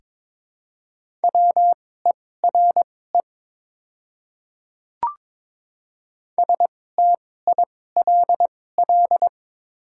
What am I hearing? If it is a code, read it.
Morse code, heard as a keyed tone, about 22 wpm, spaced as for 16 wpm, 700 Hz. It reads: WERE STILL
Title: 2x Character Spacing - Top 500 Words - 22wpm